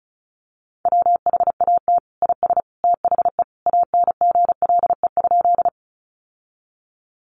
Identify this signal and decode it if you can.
Morse code, heard as a keyed tone, about 34 wpm, 700 Hz. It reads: WHAT IS THE ANGLE?
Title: Sentences from Top 1500 Words 35wpm